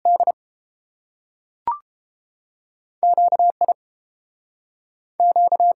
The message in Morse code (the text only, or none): D Q I Q